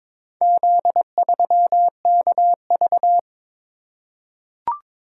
Morse code, heard as a keyed tone, about 22 wpm, 700 Hz. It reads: Z3KV
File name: practice / 22wpm / Four Letter-Number Combinations - 22wpm